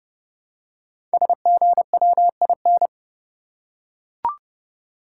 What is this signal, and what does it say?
Morse code, heard as a keyed tone, about 30 wpm, 700 Hz. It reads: SGWIN
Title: Code Groups - 5 Letters - 30wpm